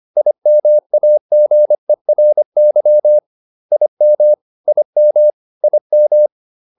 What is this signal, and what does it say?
Morse code, heard as a keyed tone, about 25 wpm, 600 Hz. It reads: IMAGERY IM IM IM